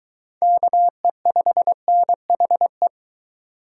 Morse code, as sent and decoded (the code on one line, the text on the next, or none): -.- . ..... -. .... .
KE5NHE